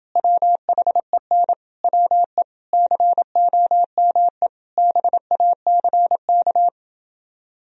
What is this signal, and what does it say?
Morse code, heard as a keyed tone, about 29 wpm, 700 Hz. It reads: WHEN WE COME BACK